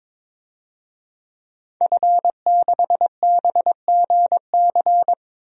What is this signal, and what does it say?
Morse code, heard as a keyed tone, about 22 wpm, 700 Hz. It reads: F6BGC
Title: DXCC-100 Call Signs - 5 Characters - 22wpm